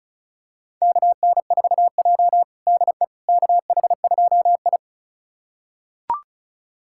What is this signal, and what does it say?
Morse code, heard as a keyed tone, about 34 wpm, 700 Hz. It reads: KN4J DE KH2I
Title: USA Callsign DE Callsign - 4 Characters - 35wpm